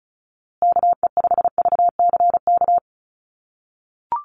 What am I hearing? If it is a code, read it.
Morse code, heard as a keyed tone, about 34 wpm, 700 Hz. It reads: KE5VCK
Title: USA Call Signs Repeated 2x 35wpm